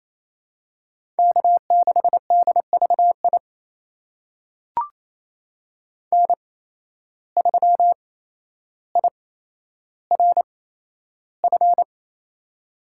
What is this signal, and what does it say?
Morse code, heard as a keyed tone, about 28 wpm, spaced as for 12 wpm, 700 Hz. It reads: K6DVI N3IRF
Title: USA Call Signs - 5 Characters - 8x Character Spacing - 28wpm